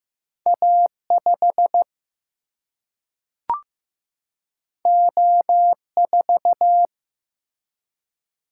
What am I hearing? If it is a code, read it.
Morse code, heard as a keyed tone, about 15 wpm, 700 Hz. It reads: A5 O4
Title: Single-Letter Single-Number 15wpm